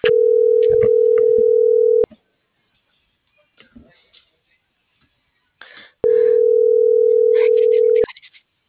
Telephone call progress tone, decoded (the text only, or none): ringback tone